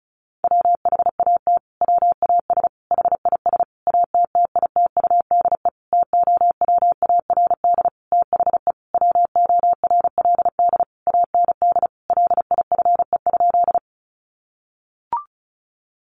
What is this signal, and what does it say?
Morse code, heard as a keyed tone, about 34 wpm, 700 Hz. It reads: WHAT WAS HIS ATTITUDE TOWARD THE WORLD AND LIFE?
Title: Sentences from Top 1300 Words 35wpm